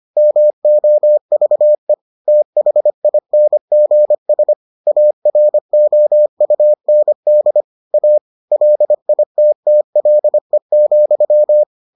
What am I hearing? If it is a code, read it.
Morse code, heard as a keyed tone, about 25 wpm, 600 Hz. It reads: MOVE THINGS AROUND A LITTLE,